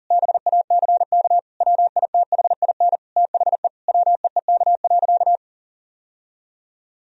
Morse code, heard as a keyed tone, about 37 wpm, 700 Hz. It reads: BACK WITHIN THE WEEK.
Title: Sentences from Top 1000 Words - Review Part 4 - 40wpm